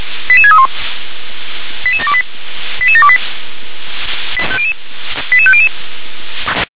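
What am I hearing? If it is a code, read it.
Five-tone selective call (ZVEI), heard as quick runs of five tones, tones of about 80 ms. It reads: 88631 88728 88728 88500
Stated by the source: ZVEI five-tone selective call received on SDR.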